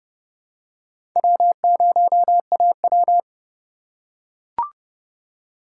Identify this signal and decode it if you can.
Morse code, heard as a keyed tone, about 28 wpm, 700 Hz. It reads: W0AW